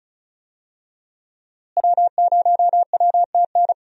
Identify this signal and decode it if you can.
Morse code, heard as a keyed tone, about 34 wpm, 700 Hz. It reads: W0WTN
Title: USA Call Signs - 5 Characters - Repeated 3x - 35wpm